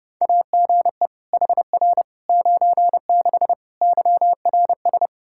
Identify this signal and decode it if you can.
Morse code, heard as a keyed tone, about 30 wpm, 700 Hz. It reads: AGE HR 96 YRS
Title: Practice QSO - Exchange 3 - More Info - 30wpm